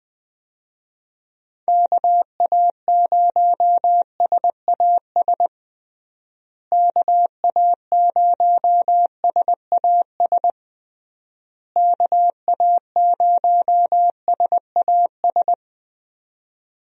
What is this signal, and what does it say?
Morse code, heard as a keyed tone, about 20 wpm, 700 Hz. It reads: KA0SAS KA0SAS KA0SAS
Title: USA Call Signs - 6 Characters - Repeated 3x - 20wpm